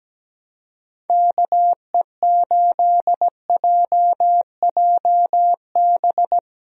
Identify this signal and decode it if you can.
Morse code, heard as a keyed tone, about 17 wpm, 700 Hz. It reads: KE8JJB
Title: USA Call Signs - 6 Characters - 17wpm